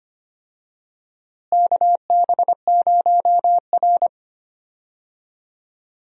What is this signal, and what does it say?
Morse code, heard as a keyed tone, about 25 wpm, 700 Hz. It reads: KB0R
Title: USA Call Signs - 4 Characters - Repeated 3x - 25wpm